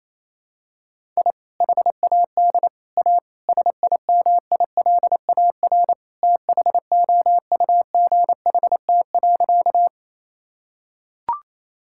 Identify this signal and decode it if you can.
Morse code, heard as a keyed tone, about 27 wpm, 700 Hz. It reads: I HAD A SIMILAR THOUGHT.
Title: Sentences from Top 1000 Words - Review Part 5 - 28wpm